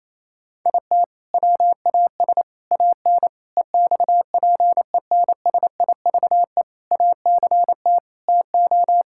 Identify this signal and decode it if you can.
Morse code, heard as a keyed tone, about 28 wpm, 700 Hz. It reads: IT WAS AN EXPENSIVE ACT TO
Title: Sentences from Top 1000 Words - Review Part 6 - 28wpm